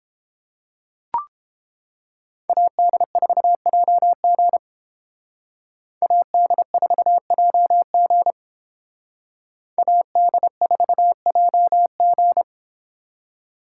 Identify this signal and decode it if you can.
Morse code, heard as a keyed tone, about 29 wpm, 700 Hz. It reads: AD4JG AD4JG AD4JG